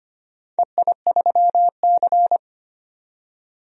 Morse code, heard as a keyed tone, about 25 wpm, 700 Hz. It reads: EI3C
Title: DXCC-100 Call Signs - 4 Characters - 25wpm